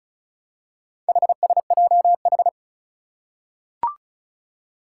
Morse code, heard as a keyed tone, about 36 wpm, 700 Hz. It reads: HSJH
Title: Four Challenging Letter-Number Combinations 35wpm